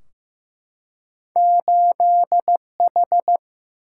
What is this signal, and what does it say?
Morse code, heard as a keyed tone, about 15 wpm, 700 Hz. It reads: 8H
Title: Single-Letter Single-Number 15wpm